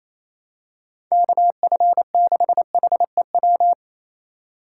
Morse code, heard as a keyed tone, about 28 wpm, 700 Hz. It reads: KF6HEW
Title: USA Call Signs - 6 Characters - 2x Character Spacing - 28wpm